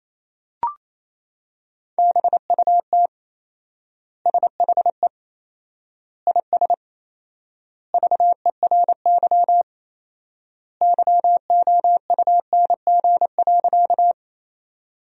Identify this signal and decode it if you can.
Morse code, heard as a keyed tone, about 28 wpm, 700 Hz. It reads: BUT SHE IS VERY YOUNG.
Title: 4x Word-Spacing - Sentences from Top 400 Words - 28wpm